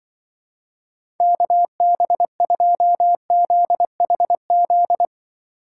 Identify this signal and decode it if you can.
Morse code, heard as a keyed tone, about 23 wpm, 700 Hz. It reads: KB2ZHZ
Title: Speed Racing - USA Call Signs 20wpm